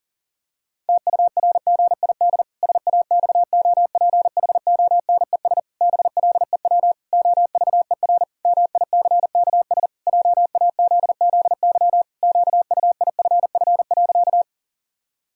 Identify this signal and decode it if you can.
Morse code, heard as a keyed tone, about 40 wpm, 700 Hz. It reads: TURGID SAXOPHONES BLEW OVER MICKS JAZZY QUIFF.
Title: Pangram Challenge v2 - 40wpm